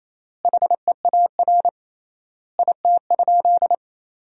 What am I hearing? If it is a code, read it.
Morse code, heard as a keyed tone, about 27 wpm, 700 Hz. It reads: HEAR IT?
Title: 3x Word-Spacing - Sentences from Top 300 Words - 28wpm